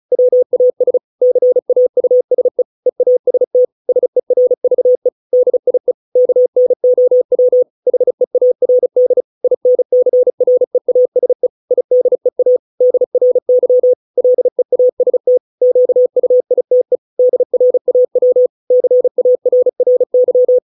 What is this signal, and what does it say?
Morse code, heard as a keyed tone, about 34 wpm, 500 Hz. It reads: WAS CAUSE EAST SERVE DIE KNOW HEARD INCREASE IDEA DRY LEAST QUITE DRAW CARRY